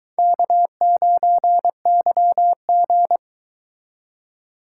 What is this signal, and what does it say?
Morse code, heard as a keyed tone, about 23 wpm, 700 Hz. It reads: K9YG